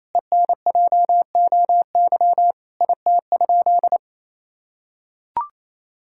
Morse code, heard as a keyed tone, about 28 wpm, 700 Hz. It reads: ENJOY IT?
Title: Sentences from Top 1000 Words - Review Part 5 - 28wpm